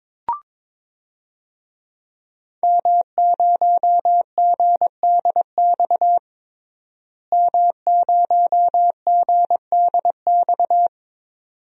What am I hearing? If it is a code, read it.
Morse code, heard as a keyed tone, about 22 wpm, 700 Hz. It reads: M0GDX M0GDX
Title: DXCC-100 Call Signs - 5 Characters - Repeated 2x - 22wpm